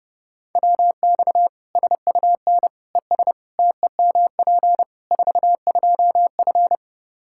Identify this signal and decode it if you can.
Morse code, heard as a keyed tone, about 30 wpm, 700 Hz. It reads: WX SUN ES TEMP 42F
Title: K1USN SST QSO Practice 30wpm